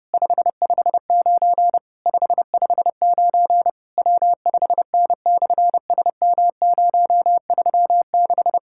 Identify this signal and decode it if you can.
Morse code, heard as a keyed tone, about 30 wpm, 700 Hz. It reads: 559 559 W5N/SM036
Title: SOTA QSO Practice 30wpm